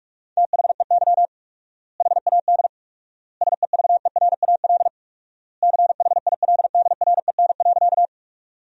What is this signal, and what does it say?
Morse code, heard as a keyed tone, about 46 wpm, 700 Hz. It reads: THEY HAD SEVERAL CHILDREN.
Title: 4x Word-Spacing - Sentences from Top 400 Words - 45wpm